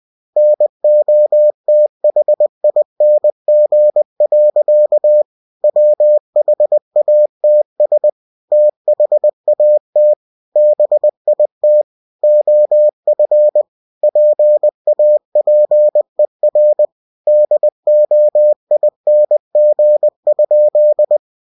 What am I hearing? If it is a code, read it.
Morse code, heard as a keyed tone, about 20 wpm, 600 Hz. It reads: NOTHING. WHATS THAT BIT OF PAPER DOING?
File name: cw-books / winnie-the-pooh / winnie-20wpm-page-0094